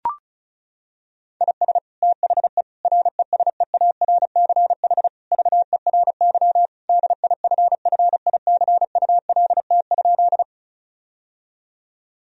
Morse code, heard as a keyed tone, about 36 wpm, 700 Hz. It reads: IS THE RESEARCH VERY DIFFICULT?
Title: Sentences from Top 1000 Words - Review Part 4 - 35wpm